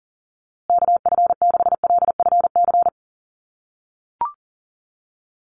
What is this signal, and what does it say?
Morse code, heard as a keyed tone, about 40 wpm, 700 Hz. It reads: KF6LFC
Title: USA Call Signs Repeated 3x 40wpm